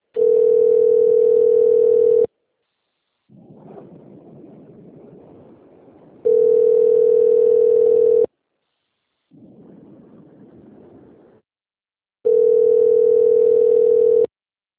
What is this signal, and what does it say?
Ringback tone (North America), 440 + 480 Hz.